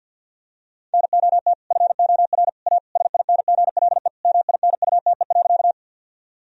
Morse code, heard as a keyed tone, about 50 wpm, 700 Hz. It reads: NOT FOR A SINGLE MINUTE.
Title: Sentences from Top 1000 Words - Review Part 4 - 50wpm